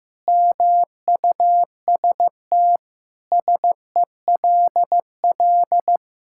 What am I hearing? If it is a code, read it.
Morse code, heard as a keyed tone, about 16 wpm, 700 Hz. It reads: MUST SELL
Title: Sentences from Top 1000 Words - Review Part 6 - 15wpm